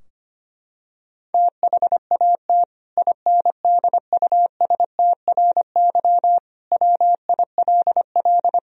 Morse code, heard as a keyed tone, about 25 wpm, 700 Hz. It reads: THAT INDUSTRY WILL
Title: Sentences from Top 1000 Words - Review Part 4 - 25wpm